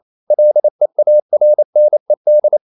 Morse code, heard as a keyed tone, about 28 wpm, 600 Hz. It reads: LEARNED
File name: CW / Practice / EnglishQuotations0228